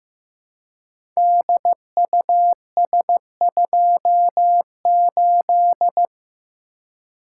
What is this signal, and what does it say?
Morse code, heard as a keyed tone, about 15 wpm, 700 Hz. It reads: DUS28
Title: Code Groups - 5 Letter:Numbers - 15wpm